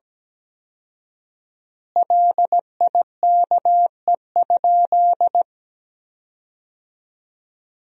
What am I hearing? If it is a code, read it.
Morse code, heard as a keyed tone, about 17 wpm, 700 Hz. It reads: LIKE?
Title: 4x Word-Spacing - Sentences from Top 100 Words - 17wpm